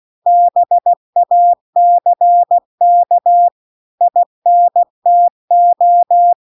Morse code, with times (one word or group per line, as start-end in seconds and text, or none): 0.3-3.5 BACK
4.0-6.3 INTO